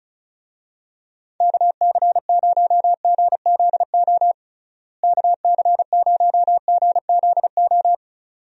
Morse code, heard as a keyed tone, about 36 wpm, 700 Hz. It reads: KC0GZO KC0GZO
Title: USA Call Signs - 6 Characters - Repeated 2x - 35wpm